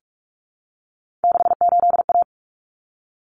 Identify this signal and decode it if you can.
Morse code, heard as a keyed tone, about 44 wpm, 700 Hz. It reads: 68A
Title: Common Cut Numbers - 3-Digit 45wpm